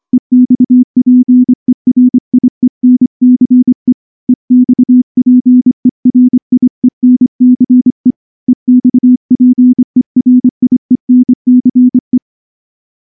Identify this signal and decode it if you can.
Morse code, heard as a keyed tone, about 24 wpm, 250 Hz. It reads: EXPERIENCE EXPERIENCE EXPERIENCE
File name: morse / morse-training-25WPM